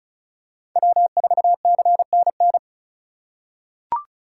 Morse code, heard as a keyed tone, about 34 wpm, 700 Hz. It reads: W4CNN
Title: USA Call Signs - 4x Character Spacing - 35wpm